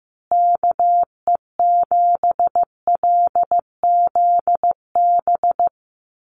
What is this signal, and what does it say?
Morse code, heard as a keyed tone, about 15 wpm, 700 Hz. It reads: KE7LZB
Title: USA Call Signs Repeated 2x 15wpm